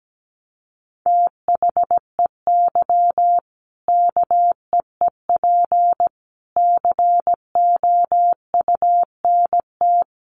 Morse code, heard as a keyed tone, about 17 wpm, 700 Hz. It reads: THEY KEEP COUNT
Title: Sentences from Top 1500 Words 17wpm